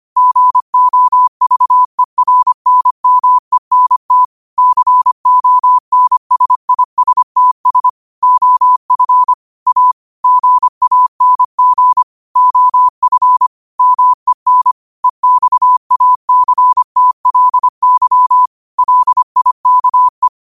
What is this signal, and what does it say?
Morse code, heard as a keyed tone, about 25 wpm, 1000 Hz. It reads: GOVERNMENT CONSISTS OF A GANG OF MEN EXACTLY LIKE
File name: qotdfiles / 25wpm